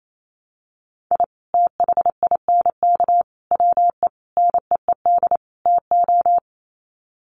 Morse code, heard as a keyed tone, about 28 wpm, 700 Hz. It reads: I THINK WE NEED TO
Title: Sets of 5 Words 28wpm